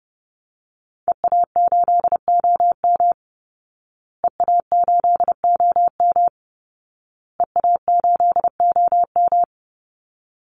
Morse code, heard as a keyed tone, about 30 wpm, 700 Hz. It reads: EA8OM EA8OM EA8OM
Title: Call Signs Repeated 2x - 30wpm